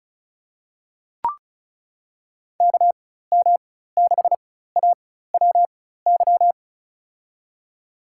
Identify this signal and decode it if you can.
Morse code, heard as a keyed tone, about 34 wpm, spaced as for 16 wpm, 700 Hz. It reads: KM6AWY
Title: USA Call Signs - 6 Characters - 4x Character Spacing - 35wpm